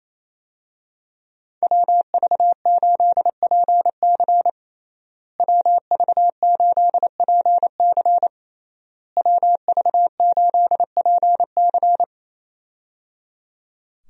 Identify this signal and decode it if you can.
Morse code, heard as a keyed tone, about 28 wpm, 700 Hz. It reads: WV8PC WV8PC WV8PC
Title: USA Call Signs - 5 Characters - Repeated 3x - 28wpm